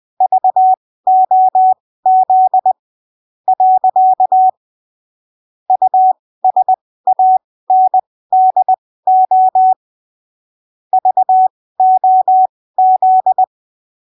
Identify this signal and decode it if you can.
Morse code, heard as a keyed tone, about 20 wpm, spaced as for 15 wpm, 750 Hz. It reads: VOZ . USANDO VOZ
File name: arquivos / texto / texto 04 20ppm